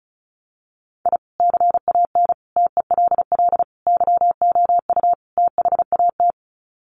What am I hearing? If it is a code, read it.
Morse code, heard as a keyed tone, about 34 wpm, 700 Hz. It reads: I CAN TELL YOU THAT
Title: Sets of 5 Words 35wpm